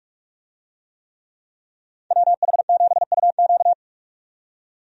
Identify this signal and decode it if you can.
Morse code, heard as a keyed tone, about 46 wpm, 700 Hz. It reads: WH7UQ
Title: USA Call Signs - 5 Characters - 45wpm